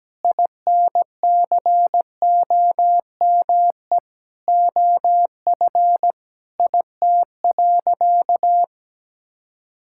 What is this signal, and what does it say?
Morse code, heard as a keyed tone, about 17 wpm, 700 Hz. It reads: INCOME OF IT.
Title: Sentences from Top 1000 Words - Review Part 6 - 17wpm